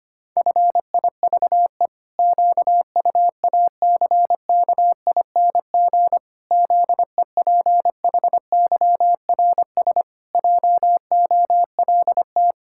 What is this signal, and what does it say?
Morse code, heard as a keyed tone, about 25 wpm, 700 Hz. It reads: FIVE QUACKING ZEPHYRS JOLT
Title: Pangram Challenge v2 - 25wpm